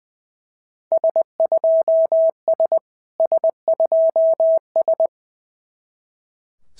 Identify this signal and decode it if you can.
Morse code, heard as a keyed tone, about 20 wpm, 650 Hz. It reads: S2S S2S
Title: SOTA QSO Practice 20wpm